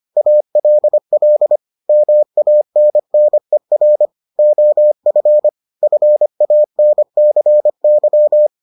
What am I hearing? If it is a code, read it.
Morse code, heard as a keyed tone, about 25 wpm, 600 Hz. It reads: ALL MANNER OF FANCY